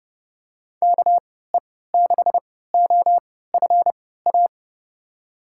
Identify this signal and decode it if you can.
Morse code, heard as a keyed tone, about 30 wpm, spaced as for 17 wpm, 700 Hz. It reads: KE6OFA
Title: USA Call Signs - 6 Characters - 3x Character Spacing - 30wpm